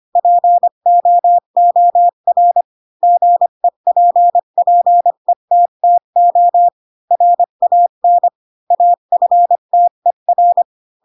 Morse code, heard as a keyed tone, about 25 wpm, 700 Hz. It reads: POOR GEPPETTO RAN AFTER
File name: Pinocchio / Pinocchio ch3 20 25 ik1hge